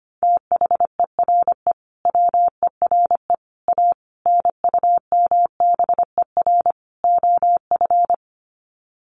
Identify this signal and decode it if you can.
Morse code, heard as a keyed tone, about 25 wpm, 700 Hz. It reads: THERE WERE A NUMBER OF
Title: Sets of 5 Words Encore2 25wpm